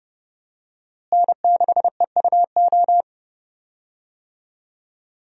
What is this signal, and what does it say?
Morse code, heard as a keyed tone, about 30 wpm, 700 Hz. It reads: N6EUO
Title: USA Call Signs - 5 Characters - Repeated 3x - 30wpm